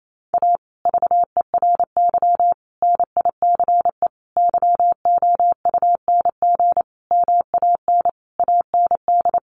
Morse code, heard as a keyed tone, about 28 wpm, 700 Hz. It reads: A VERY NICE YOUNG MAN AND
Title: Sentences from Top 1200 Words 28wpm